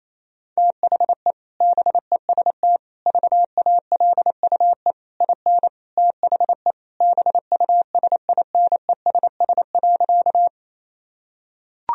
Morse code, heard as a keyed tone, about 28 wpm, 700 Hz. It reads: THE BEST VALUE IN THE BUSINESS.
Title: Sentences from Top 1000 Words - Review Part 3 - 28wpm